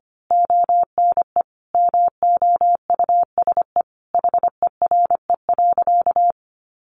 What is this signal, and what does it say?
Morse code, heard as a keyed tone, about 25 wpm, 700 Hz. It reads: ONE MOUSE HERE.
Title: A Sound Of Thunder 25wpm